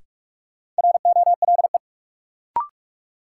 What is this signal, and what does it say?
Morse code, heard as a keyed tone, about 45 wpm, 700 Hz. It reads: ROLE
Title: Sentences from Top 1000 Words - Review Part 6 - New Word Drill - 45wpm